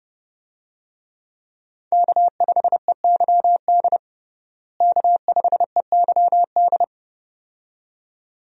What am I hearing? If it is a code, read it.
Morse code, heard as a keyed tone, about 30 wpm, 700 Hz. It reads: K5EYD K5EYD
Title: USA Call Signs - 5 Characters - Repeated 2x - 30wpm